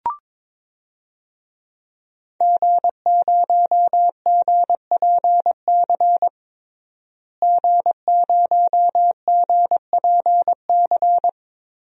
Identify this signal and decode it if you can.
Morse code, heard as a keyed tone, about 22 wpm, 700 Hz. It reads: G0GPC G0GPC
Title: DXCC-100 Call Signs - 5 Characters - Repeated 3x - 22wpm